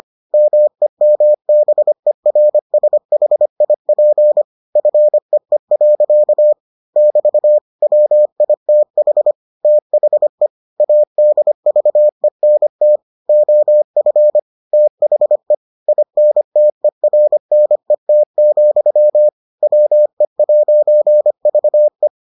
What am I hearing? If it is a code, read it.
Morse code, heard as a keyed tone, about 25 wpm, 600 Hz. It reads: MEMBERSHIP FEE. <BT> WITH THE ADVENT OF THE INTERNET, WE'VE